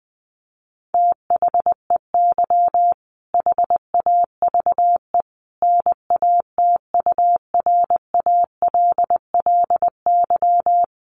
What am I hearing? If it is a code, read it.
Morse code, heard as a keyed tone, about 20 wpm, 700 Hz. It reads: THEY HAVE NATURALLY
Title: Sentences from Top 1300 Words 20wpm